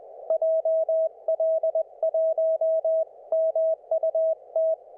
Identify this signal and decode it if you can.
Morse code, heard as a keyed tone, about 20 wpm, 640 Hz. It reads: JL1MUT